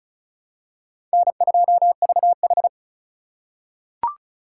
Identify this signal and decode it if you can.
Morse code, heard as a keyed tone, about 34 wpm, 700 Hz. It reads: N2VH
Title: USA Call Signs - 4 Characters - 4x Character Spacing - 35wpm